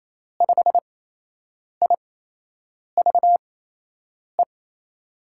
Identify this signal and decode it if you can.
Morse code, heard as a keyed tone, about 27 wpm, spaced as for 8 wpm, 700 Hz. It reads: 5IVE
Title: USA Call Signs - 6 Characters - 8x Character Spacing - 28wpm